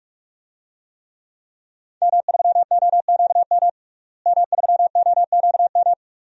Morse code, heard as a keyed tone, about 46 wpm, 700 Hz. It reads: M3OQM M3OQM
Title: DXCC-100 Call Signs - 5 Characters - Repeated 3x - 45wpm